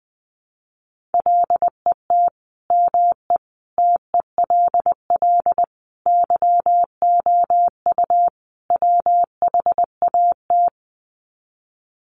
Morse code, heard as a keyed tone, about 20 wpm, 700 Hz. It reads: LET ME TELL YOU WHAT
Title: Sets of 5 Words Encore 20wpm